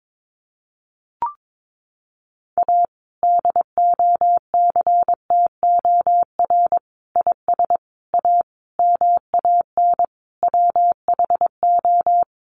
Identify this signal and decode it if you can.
Morse code, heard as a keyed tone, about 22 wpm, 700 Hz. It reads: A DOCTOR IS A MAN WHO
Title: Sentences from Top 1500 Words 22wpm